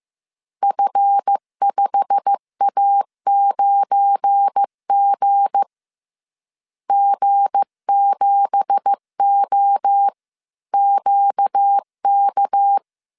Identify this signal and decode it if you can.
Morse code, heard as a keyed tone, about 15 wpm, 800 Hz. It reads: F5A9G G7OQK
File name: CW / DET / 1 9 1500C S C 14wpm 3245